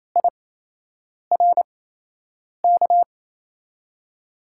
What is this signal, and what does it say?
Morse code, heard as a keyed tone, about 28 wpm, spaced as for 8 wpm, 700 Hz. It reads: IRK
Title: USA Call Signs - 6 Characters - 8x Character Spacing - 28wpm